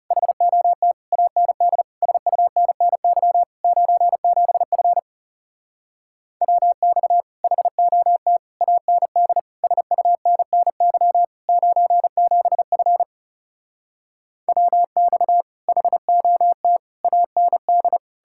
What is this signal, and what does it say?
Morse code, heard as a keyed tone, about 40 wpm, 700 Hz. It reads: HOT AND SUNNY 97F WX HOT AND SUNNY 97F WX HOT AND